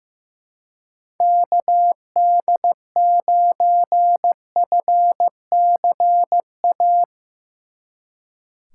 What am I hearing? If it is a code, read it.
Morse code, heard as a keyed tone, about 15 wpm, 700 Hz. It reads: KD9FCA
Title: Speed Racing - USA Call Signs 15wpm